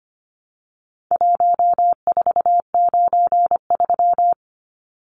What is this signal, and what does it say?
Morse code, heard as a keyed tone, about 25 wpm, 700 Hz. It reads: 1493
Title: Numbers - Contest Style - 4-Digit 25wpm